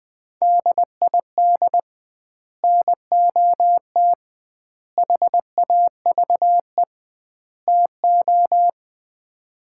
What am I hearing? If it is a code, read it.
Morse code, heard as a keyed tone, about 20 wpm, 700 Hz. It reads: DID NOT HAVE TO
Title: 2x Word-Spacing - Sets of 4 Words - 20wpm